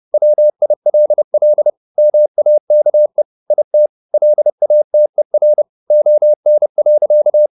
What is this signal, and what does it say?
Morse code, heard as a keyed tone, about 30 wpm, 600 Hz. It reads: WILL MAKE IT LATER ON.
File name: the old man and the sea / the old man and the sea 1 30wpm